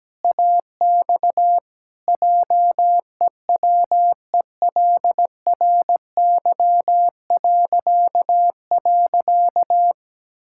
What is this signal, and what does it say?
Morse code, heard as a keyed tone, about 17 wpm, 700 Hz. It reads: AX JEWELRY..
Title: Pangram Challenge v2 - 17wpm